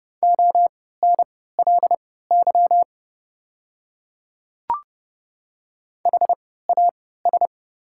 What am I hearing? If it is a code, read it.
Morse code, heard as a keyed tone, about 30 wpm, spaced as for 17 wpm, 700 Hz. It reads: ONLY HAS
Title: 3x Character Spacing - Top 200 Words - 30wpm